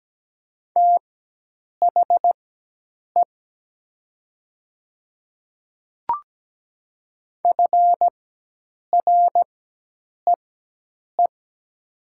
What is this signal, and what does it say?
Morse code, heard as a keyed tone, about 17 wpm, spaced as for 8 wpm, 700 Hz. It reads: THE FREE
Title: 4x Character Spacing - Top 500 Words - 17wpm